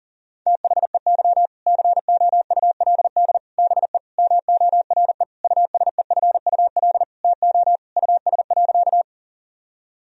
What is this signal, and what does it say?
Morse code, heard as a keyed tone, about 40 wpm, 700 Hz. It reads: THEY COULD BE MORE USEFUL TO US.
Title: Sentences from Top 1000 Words - Review Part 4 - 40wpm